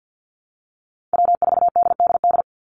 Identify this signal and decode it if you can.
Morse code, heard as a keyed tone, about 50 wpm, 700 Hz. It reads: W4DDD
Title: Call Signs - 50wpm